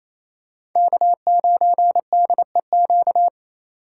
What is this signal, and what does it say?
Morse code, heard as a keyed tone, about 27 wpm, 700 Hz. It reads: K9DEQ
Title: USA Call Signs - 5 Characters - 2x Character Spacing - 28wpm